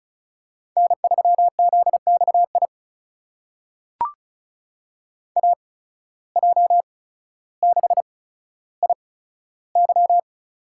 Morse code, heard as a keyed tone, about 36 wpm, spaced as for 15 wpm, 700 Hz. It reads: N3ZXI AJ6IY
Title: USA Call Signs - 5 Characters - 8x Character Spacing - 35wpm